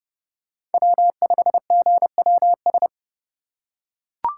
Morse code, heard as a keyed tone, about 32 wpm, 700 Hz. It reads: W5GWS
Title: USA Call Signs - 5 Characters - 4x Character Spacing - 30wpm